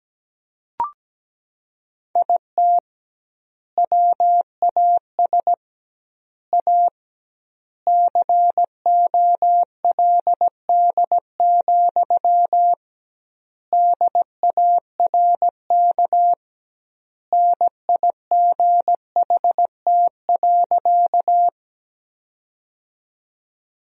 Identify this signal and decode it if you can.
Morse code, heard as a keyed tone, about 17 wpm, 700 Hz. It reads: IT WAS A COLD, DARK NIGHT.
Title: 2x Word-Spacing - Sentences from Top 500 Words - 17wpm